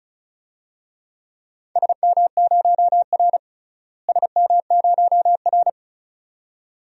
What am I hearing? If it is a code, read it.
Morse code, heard as a keyed tone, about 34 wpm, 700 Hz. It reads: SM0R SM0R
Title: DXCC-100 Call Signs - 4 Characters - Repeated 2x - 35wpm